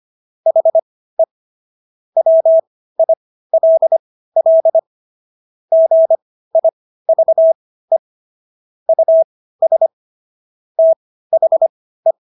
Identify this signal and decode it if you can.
Morse code, heard as a keyed tone, about 25 wpm, spaced as for 15 wpm, 650 Hz. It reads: HE WILL GIVE US THE